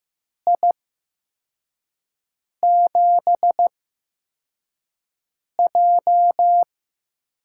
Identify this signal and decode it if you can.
Morse code, heard as a keyed tone, about 15 wpm, spaced as for 4 wpm, 700 Hz. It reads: I7J
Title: USA Call Signs - 6 Characters - 8x Character Spacing - 15wpm